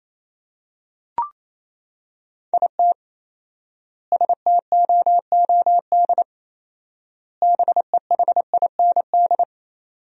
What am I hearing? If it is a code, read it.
Morse code, heard as a keyed tone, about 28 wpm, 700 Hz. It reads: IT STOOD BEHIND
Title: 4x Word-Spacing - Sentences from Top 500 Words - 28wpm